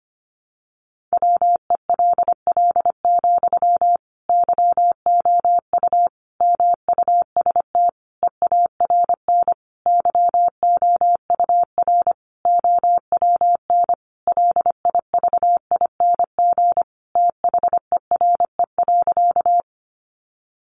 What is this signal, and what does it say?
Morse code, heard as a keyed tone, about 25 wpm, 700 Hz. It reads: WELL, YOU MUST EARN YOUR OWN LIVING THERE.
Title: Sentences from Top 1500 Words 25wpm